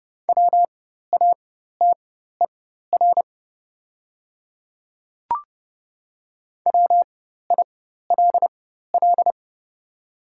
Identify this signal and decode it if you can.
Morse code, heard as a keyed tone, about 30 wpm, spaced as for 14 wpm, 700 Hz. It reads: WATER WILL
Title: 4x Character Spacing - Top 200 Words - 30wpm